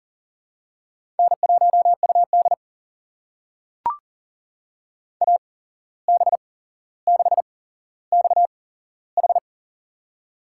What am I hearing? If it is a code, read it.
Morse code, heard as a keyed tone, about 40 wpm, spaced as for 16 wpm, 700 Hz. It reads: N1UD AB6XH
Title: USA Call Signs - 8x Character Spacing - 40wpm